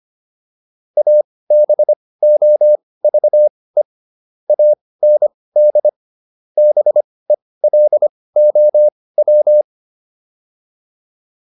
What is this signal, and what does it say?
Morse code, heard as a keyed tone, about 25 wpm, spaced as for 18 wpm, 600 Hz. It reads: ABOVE AND BELOW